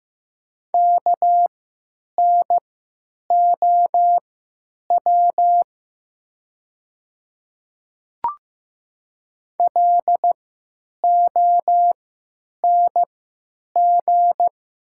Morse code, heard as a keyed tone, about 15 wpm, spaced as for 9 wpm, 700 Hz. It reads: KNOW LONG
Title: 3x Character Spacing - Top 100 Words - 15wpm